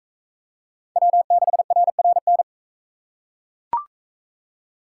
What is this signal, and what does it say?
Morse code, heard as a keyed tone, about 41 wpm, 700 Hz. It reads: W6RRN